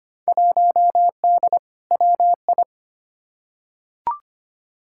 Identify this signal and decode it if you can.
Morse code, heard as a keyed tone, about 25 wpm, 700 Hz. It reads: 1D WI
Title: Field Day Exchanges - 25wpm